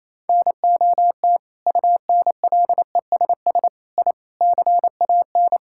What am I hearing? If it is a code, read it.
Morse code, heard as a keyed tone, about 28 wpm, 700 Hz. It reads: NOT UNLESS I CAN
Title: Sentences from Top 1000 Words - Review Part 6 - 28wpm